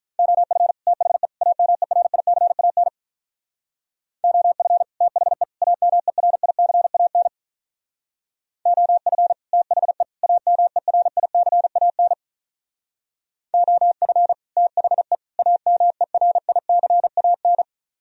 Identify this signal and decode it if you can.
Morse code, heard as a keyed tone, about 44 wpm, 700 Hz. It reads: OF THE AMERICAN OF THE AMERICAN OF THE AMERICAN OF THE AMERICAN